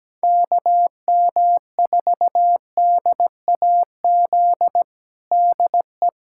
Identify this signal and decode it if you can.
Morse code, heard as a keyed tone, about 17 wpm, 700 Hz. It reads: KM4DAZ DE
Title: USA Callsign DE Callsign 17wpm